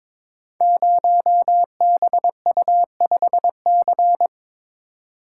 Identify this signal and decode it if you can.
Morse code, heard as a keyed tone, about 22 wpm, 700 Hz. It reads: 0BU5C
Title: Code Groups - 5 Letter:Numbers - 22wpm